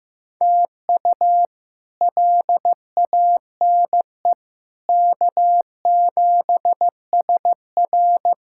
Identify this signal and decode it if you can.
Morse code, heard as a keyed tone, about 15 wpm, 700 Hz. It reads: TU LANE K7SR
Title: K1USN SST QSO Practice - 15wpm